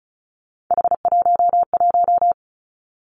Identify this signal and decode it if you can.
Morse code, heard as a keyed tone, about 36 wpm, 700 Hz. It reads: H11
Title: Three Challenging Letter-Number Combinations 35wpm